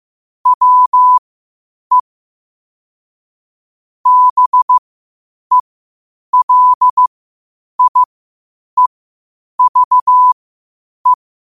Morse code, heard as a keyed tone, about 15 wpm, spaced as for 8 wpm, 1000 Hz. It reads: WE BELIEVE
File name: qotdfiles / 8wpm